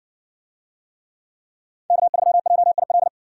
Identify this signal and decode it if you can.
Morse code, heard as a keyed tone, about 60 wpm, 700 Hz. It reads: K3JIL